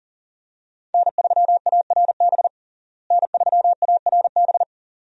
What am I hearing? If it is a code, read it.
Morse code, heard as a keyed tone, about 40 wpm, 700 Hz. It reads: N3ARB N3ARB
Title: USA Call Signs - 5 Characters - Repeated 2x - 40wpm